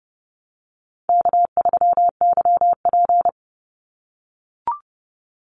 Morse code, heard as a keyed tone, about 30 wpm, 700 Hz. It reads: K3YP
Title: Call Signs - 30wpm